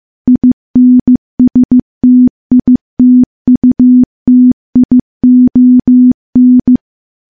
Morse code, heard as a keyed tone, about 15 wpm, 250 Hz. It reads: INSTITUTION
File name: morse / morse-training-15WPM